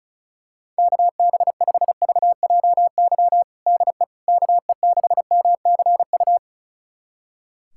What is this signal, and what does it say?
Morse code, heard as a keyed tone, about 34 wpm, 700 Hz. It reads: KB5VJY DE KE6MCU
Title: USA Callsign DE Callsign 35wpm